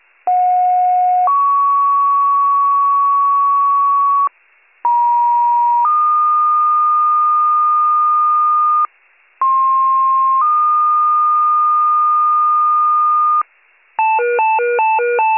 Two-tone page, heard as a one-second tone, then a three-second tone, about 730 then 1120 Hz, about 950 then 1220 Hz, about 1060 then 1220 Hz. It is heard from 0.3 to 13.4 s.